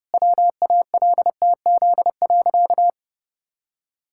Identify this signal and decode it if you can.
Morse code, heard as a keyed tone, about 32 wpm, 700 Hz. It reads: WALTZ.
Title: Pangram Challenge v2 - 30wpm